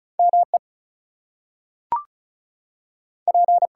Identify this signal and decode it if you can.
Morse code, heard as a keyed tone, about 36 wpm, spaced as for 6 wpm, 700 Hz. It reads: MEP
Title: US States and Canadian Provinces 35wpm